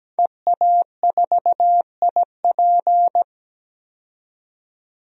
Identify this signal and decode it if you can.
Morse code, heard as a keyed tone, about 17 wpm, 700 Hz. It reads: EA4IP
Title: DXCC-100 Call Signs - 5 Characters - 17wpm